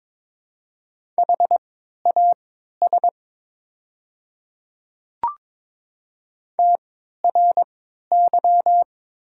Morse code, heard as a keyed tone, about 23 wpm, spaced as for 13 wpm, 700 Hz. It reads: HAS TRY
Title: 3x Character Spacing - Top 300 Words - 22wpm